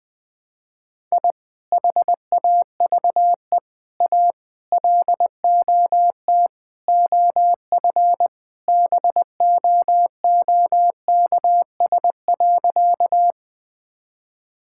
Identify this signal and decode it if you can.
Morse code, heard as a keyed tone, about 21 wpm, 700 Hz. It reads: I HAVE A LOT OF BOOKS.
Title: Sentences with 5-Word Phrases - Spoken Beginning - 20wpm